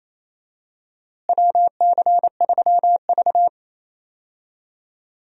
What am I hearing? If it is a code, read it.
Morse code, heard as a keyed tone, about 28 wpm, 700 Hz. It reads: WC3V
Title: USA Call Signs - 4 Characters - Repeated 3x - 28wpm